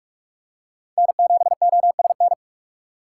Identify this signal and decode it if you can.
Morse code, heard as a keyed tone, about 45 wpm, 700 Hz. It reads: N7OSN
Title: USA Call Signs - 4x Character Spacing - 45wpm